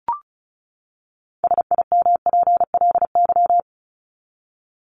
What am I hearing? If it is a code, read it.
Morse code, heard as a keyed tone, about 34 wpm, 700 Hz. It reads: SIMPLY
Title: Top 501-750 Words 35wpm